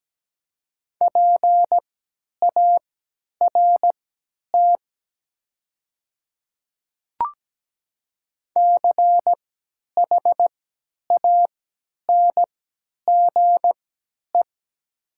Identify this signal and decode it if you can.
Morse code, heard as a keyed tone, about 17 wpm, spaced as for 10 wpm, 700 Hz. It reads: PART CHANGE
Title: 3x Character Spacing - Top 200 Words - 17wpm